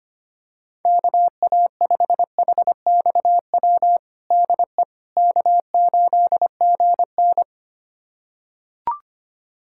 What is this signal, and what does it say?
Morse code, heard as a keyed tone, about 25 wpm, 700 Hz. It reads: KA5HXW DE K8GN
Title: USA Callsign DE Callsign 25wpm